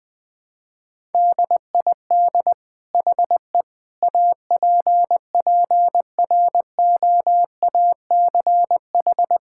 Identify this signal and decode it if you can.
Morse code, heard as a keyed tone, about 19 wpm, 700 Hz. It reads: DID HE APPROACH
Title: Sentences from Top 1000 Words - Review Part 6 - 20wpm